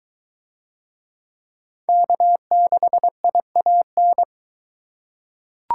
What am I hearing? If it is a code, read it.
Morse code, heard as a keyed tone, about 22 wpm, 700 Hz. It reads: K6IAN